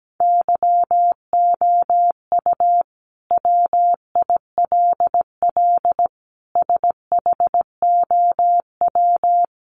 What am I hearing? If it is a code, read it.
Morse code, heard as a keyed tone, about 17 wpm, 700 Hz. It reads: YOU WILL SHOW
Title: Sentences from Top 600 Words 17wpm